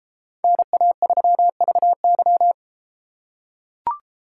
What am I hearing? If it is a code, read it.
Morse code, heard as a keyed tone, about 32 wpm, 700 Hz. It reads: NA3VY